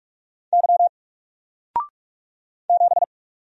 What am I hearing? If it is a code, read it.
Morse code, heard as a keyed tone, about 44 wpm, 700 Hz. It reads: Y 7
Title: Single Letter-Number 45wpm